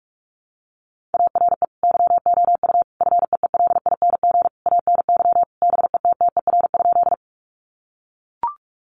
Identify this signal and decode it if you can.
Morse code, heard as a keyed tone, about 44 wpm, 700 Hz. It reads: ARE YOU FEELING ANY BETTER?
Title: Sentences from Top 1200 Words 45wpm